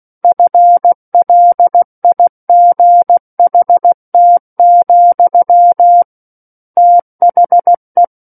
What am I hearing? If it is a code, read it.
Morse code, heard as a keyed tone, about 16 wpm, 700 Hz. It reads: FLIGHT, THE
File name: Book One / 07-Chapter 7 - 16 WPM